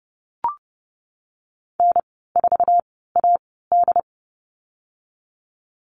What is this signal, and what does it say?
Morse code, heard as a keyed tone, about 30 wpm, spaced as for 17 wpm, 700 Hz. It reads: N4AD
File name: Lesson 14g - Number 2 - Call Signs 30wpm